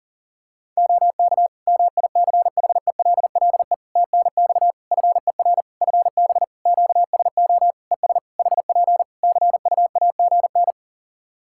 Morse code, heard as a keyed tone, about 43 wpm, 700 Hz. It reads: OK MICHELLE TNX FER FB QSO ES HP CUAGN
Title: Practice QSO - Exchange 4 - Ending - 40wpm